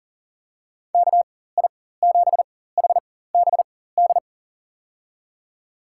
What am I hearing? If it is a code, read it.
Morse code, heard as a keyed tone, about 37 wpm, spaced as for 18 wpm, 700 Hz. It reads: KI7HBD
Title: USA Call Signs - 6 Characters - 4x Character Spacing - 40wpm